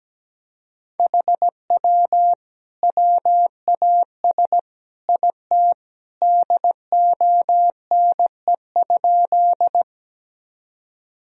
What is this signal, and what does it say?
Morse code, heard as a keyed tone, about 17 wpm, 700 Hz. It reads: HW WAS IT DONE?
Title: Sentences from Top 500 Words and Abbreviations 17wpm